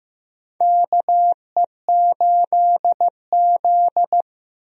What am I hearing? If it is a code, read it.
Morse code, heard as a keyed tone, about 15 wpm, 700 Hz. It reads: KE8Z
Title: USA Call Signs - 4 Characters - 15wpm